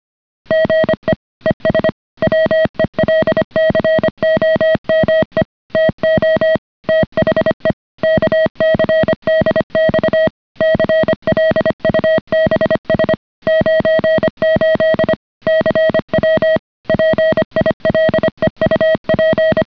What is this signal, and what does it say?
Morse code, heard as a keyed tone, about 25 wpm, 630 Hz. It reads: GE ES WELCOME TO THE KCDX CLUBS 98 CW PILEUP